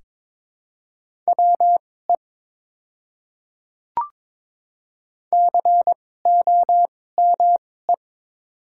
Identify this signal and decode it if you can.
Morse code, heard as a keyed tone, about 21 wpm, spaced as for 16 wpm, 700 Hz. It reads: WE COME
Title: 2x Character Spacing - Top 200 Words - 22wpm